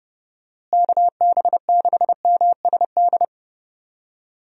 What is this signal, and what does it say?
Morse code, heard as a keyed tone, about 32 wpm, 700 Hz. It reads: KB6MSD
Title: USA Call Signs - 6 Characters - Repeated 3x - 30wpm